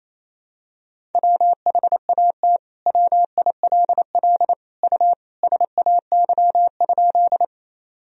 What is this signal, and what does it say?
Morse code, heard as a keyed tone, about 28 wpm, 700 Hz. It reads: WHAT WILL U SAY?
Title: Sentences from Top 200 Words and Abbreviations 28wpm